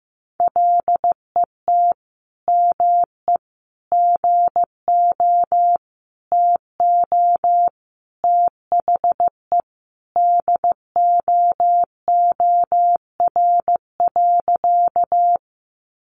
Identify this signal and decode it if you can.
Morse code, heard as a keyed tone, about 15 wpm, 700 Hz. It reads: LET ME GO TO THE DOOR.
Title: Sentences from Top 300 Words 15wpm